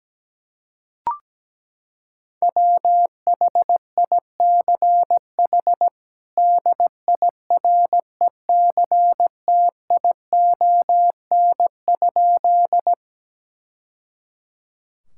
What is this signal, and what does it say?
Morse code, heard as a keyed tone, about 17 wpm, 700 Hz. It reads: WHICH DIRECTION?
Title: Sentences from Top 1000 Words - Review Part 5 - 17wpm